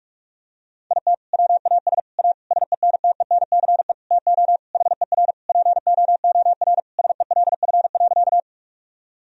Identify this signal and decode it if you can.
Morse code, heard as a keyed tone, about 42 wpm, 700 Hz. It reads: IT WAS A SENTENCE TO HER POOR SELF.
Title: Sentences from Top 1000 Words - Review Part 1 - 45wpm